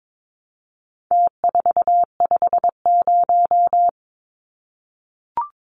Morse code, heard as a keyed tone, about 22 wpm, 700 Hz. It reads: T450
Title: Numbers - Contest Style - 4-Digit 22wpm